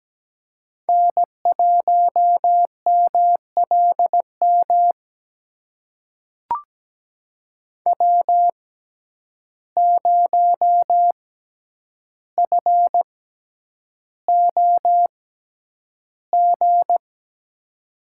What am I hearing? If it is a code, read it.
Morse code, heard as a keyed tone, about 17 wpm, spaced as for 9 wpm, 700 Hz. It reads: N1MLM W0FOG
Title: USA Call Signs - 5 Characters - 6x Character Spacing - 17wpm